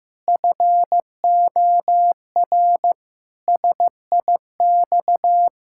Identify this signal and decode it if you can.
Morse code, heard as a keyed tone, about 15 wpm, 700 Hz. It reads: FOR SIX
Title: Pangram Challenge v2 - 15wpm